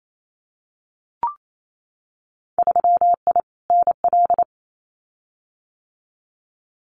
Morse code, heard as a keyed tone, about 28 wpm, 700 Hz. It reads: 3I NL
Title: Winter Field Day Exchanges 28wpm